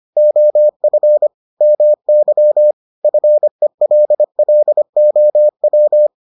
Morse code, heard as a keyed tone, about 25 wpm, 600 Hz. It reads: OF MY FELLOW